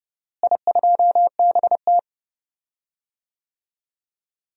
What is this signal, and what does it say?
Morse code, heard as a keyed tone, about 31 wpm, 700 Hz. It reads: I2BT
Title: DXCC-100 Call Signs - 4 Characters - Repeated 2x - 30wpm